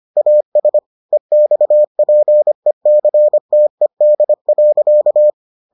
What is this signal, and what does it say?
Morse code, heard as a keyed tone, about 25 wpm, 600 Hz. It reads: AS EXPECTED.